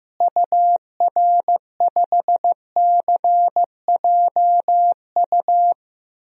Morse code, heard as a keyed tone, about 15 wpm, 700 Hz. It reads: UR5CJU
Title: DXCC-100 Call Signs - 6 Characters - Repeated 2x - 15wpm